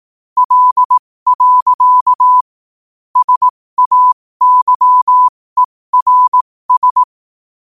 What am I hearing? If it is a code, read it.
Morse code, heard as a keyed tone, about 18 wpm, spaced as for 15 wpm, 1000 Hz. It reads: L. SAYERS